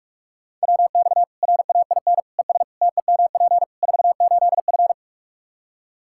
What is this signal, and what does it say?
Morse code, heard as a keyed tone, about 43 wpm, 700 Hz. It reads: WX RAIN ES TEMP 48F
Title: Practice QSO - Exchange 2 - Info - 45wpm